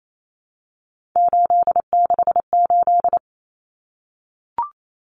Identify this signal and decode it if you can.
Morse code, heard as a keyed tone, about 28 wpm, 700 Hz. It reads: 868
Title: Numbers - 3-Digit 28wpm